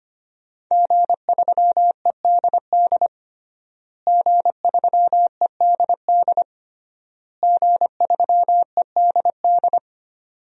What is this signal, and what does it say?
Morse code, heard as a keyed tone, about 25 wpm, 700 Hz. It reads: G3EDD G3EDD G3EDD
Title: DXCC-100 Call Signs - 5 Characters - Repeated 3x - 25wpm